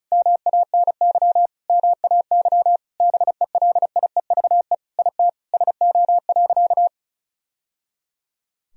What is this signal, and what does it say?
Morse code, heard as a keyed tone, about 34 wpm, 700 Hz. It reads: MANY MAY BELIEVE IT SO.
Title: Sentences from Top 1000 Words - Review Part 4 - 35wpm